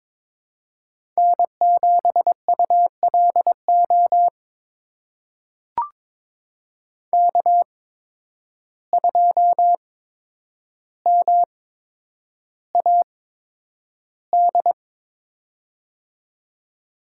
Morse code, heard as a keyed tone, about 22 wpm, spaced as for 9 wpm, 700 Hz. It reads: N7ULO K2MAD
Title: USA Call Signs - 5 Characters - 8x Character Spacing - 22wpm